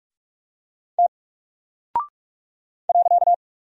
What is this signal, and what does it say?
Morse code, heard as a keyed tone, about 47 wpm, 700 Hz. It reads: T .
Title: Single Character (Letter:Number:Punctuation:Prosign) Alt 45wpm